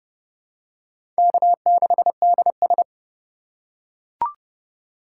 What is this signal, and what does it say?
Morse code, heard as a keyed tone, about 30 wpm, 700 Hz. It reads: K6DS
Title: USA Call Signs - 4 Characters - 3x Character Spacing - 30wpm